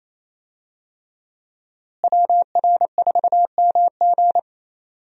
Morse code, heard as a keyed tone, about 28 wpm, 700 Hz. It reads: WR4MG
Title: USA Call Signs - 5 Characters - Repeated 3x - 28wpm